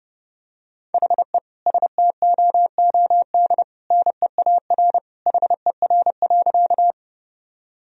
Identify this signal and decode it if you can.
Morse code, heard as a keyed tone, about 32 wpm, 700 Hz. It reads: HE STOOD NEAR HER.
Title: Sentences from Top 1000 Words - Review Part 2 - 30wpm